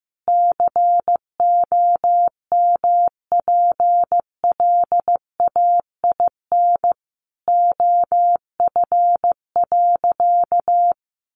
Morse code, heard as a keyed tone, about 14 wpm, 700 Hz. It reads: COMPLAIN OF.
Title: Sentences from Top 1300 Words 15wpm